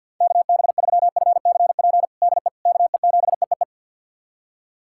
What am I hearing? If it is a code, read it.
Morse code, heard as a keyed tone, about 51 wpm, 700 Hz. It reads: KB3FCP DE KE7EEE
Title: USA Callsign DE Callsign 50wpm